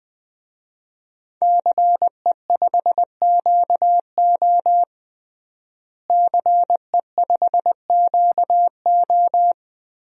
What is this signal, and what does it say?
Morse code, heard as a keyed tone, about 21 wpm, 700 Hz. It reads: CE5QO CE5QO
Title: DXCC-100 Call Signs - 5 Characters - Repeated 2x - 20wpm